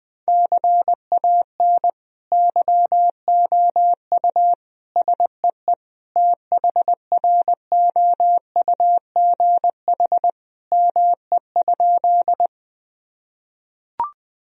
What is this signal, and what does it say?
Morse code, heard as a keyed tone, about 20 wpm, 700 Hz. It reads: CAN YOU SEE THROUGH ME?
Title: Sentences from Top 1000 Words - Review Part 1 - 20wpm